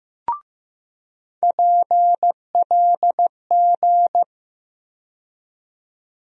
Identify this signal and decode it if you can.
Morse code, heard as a keyed tone, about 15 wpm, 700 Hz. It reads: PLG
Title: USA Call Sign Postfixes - 3 Letters 15wpm